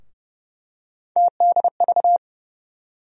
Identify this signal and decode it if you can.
Morse code, heard as a keyed tone, about 32 wpm, 700 Hz. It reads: TDV
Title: USA Call Sign Postfixes - 3 Letters 30wpm